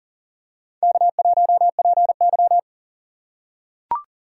Morse code, heard as a keyed tone, about 40 wpm, 700 Hz. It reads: K1PY
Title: USA Call Signs - 4 Characters - 3x Character Spacing - 40wpm